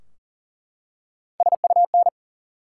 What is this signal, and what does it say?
Morse code, heard as a keyed tone, about 38 wpm, 700 Hz. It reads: SUN
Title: Three Letter Words - 40wpm